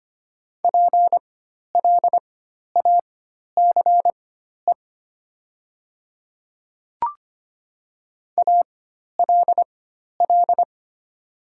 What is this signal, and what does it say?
Morse code, heard as a keyed tone, about 25 wpm, spaced as for 12 wpm, 700 Hz. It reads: PLACE ALL
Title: 4x Character Spacing - Top 200 Words - 25wpm